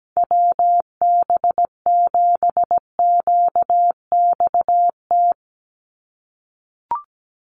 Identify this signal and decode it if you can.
Morse code, heard as a keyed tone, about 17 wpm, 700 Hz. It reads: WB7QXT
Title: USA Call Signs 17wpm